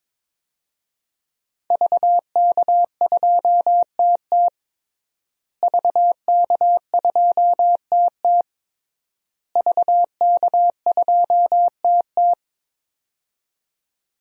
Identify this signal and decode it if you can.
Morse code, heard as a keyed tone, about 22 wpm, 700 Hz. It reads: VK2TT VK2TT VK2TT
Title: DXCC-100 Call Signs - 5 Characters - Repeated 3x - 22wpm